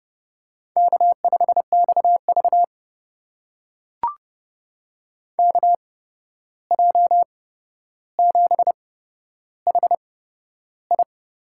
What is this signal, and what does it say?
Morse code, heard as a keyed tone, about 30 wpm, spaced as for 12 wpm, 700 Hz. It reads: K5XV KJ7HI
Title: USA Call Signs - 8x Character Spacing - 30wpm